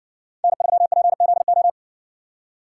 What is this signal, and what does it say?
Morse code, heard as a keyed tone, about 60 wpm, 700 Hz. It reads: N3PZO